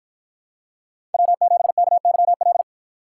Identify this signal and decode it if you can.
Morse code, heard as a keyed tone, about 53 wpm, 700 Hz. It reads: W7KYL